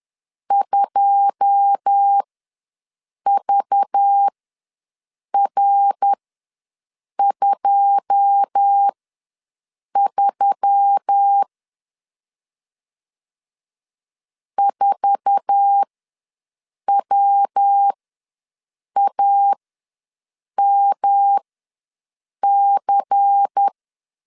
Morse code, heard as a keyed tone, about 10 wpm, spaced as for 6 wpm, 800 Hz. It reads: 2VR23 4WAMC
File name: LEZIONI / 9 2 ws 10wpm 4130